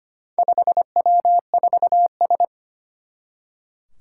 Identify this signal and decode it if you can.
Morse code, heard as a keyed tone, about 25 wpm, 700 Hz. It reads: 5W4S
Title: Four Challenging Letter-Number Combinations 25wpm